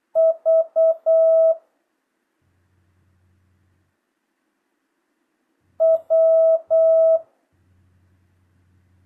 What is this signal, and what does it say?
Morse code, heard as a keyed tone, about 8 wpm, 650 Hz. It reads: V W
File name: jh8cbh / 2025cw